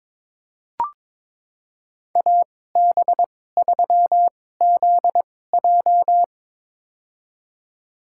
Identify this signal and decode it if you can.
Morse code, heard as a keyed tone, about 21 wpm, spaced as for 16 wpm, 700 Hz. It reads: AB3ZJ
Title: USA Call Signs - 5 Characters - 2x Character Spacing - 22wpm